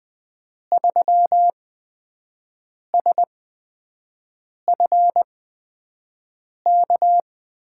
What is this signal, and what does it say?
Morse code, heard as a keyed tone, about 20 wpm, spaced as for 5 wpm, 700 Hz. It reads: 3SFK
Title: USA Call Signs - 8x Character Spacing - 20wpm